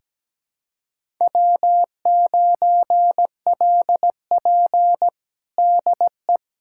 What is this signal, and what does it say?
Morse code, heard as a keyed tone, about 17 wpm, 700 Hz. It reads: W9LP DE
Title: USA Callsign DE Callsign - 4 Characters - 17wpm